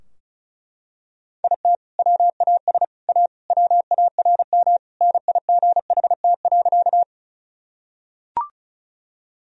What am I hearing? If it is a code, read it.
Morse code, heard as a keyed tone, about 36 wpm, 700 Hz. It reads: IT WAS A WARM NIGHT.
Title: Sentences from Top 1000 Words - Review Part 2 - 35wpm